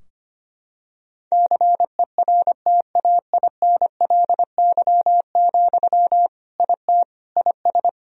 Morse code, heard as a keyed tone, about 25 wpm, 700 Hz. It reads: CERTAINLY, IT IS
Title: Sentences from Top 1000 Words - Review Part 2 - 25wpm